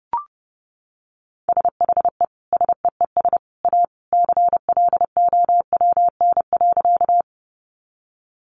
Morse code, heard as a keyed tone, about 30 wpm, 700 Hz. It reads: SHE SEES A CLOWN.
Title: Easy Phrases 30wpm